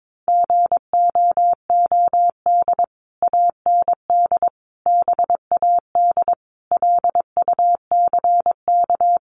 Morse code, heard as a keyed tone, about 22 wpm, 700 Hz. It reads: GOOD AND BAD LUCK
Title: Sentences from Top 1500 Words 22wpm